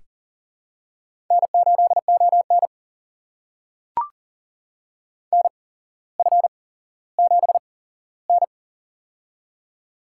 Morse code, heard as a keyed tone, about 40 wpm, spaced as for 17 wpm, 700 Hz. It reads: N8ON NF7N
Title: USA Call Signs - 4 Characters - 8x Character Spacing - 40wpm